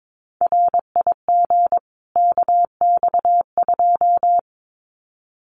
Morse code, heard as a keyed tone, about 22 wpm, 700 Hz. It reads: RIG KX2
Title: Common QSO Phrases 22wpm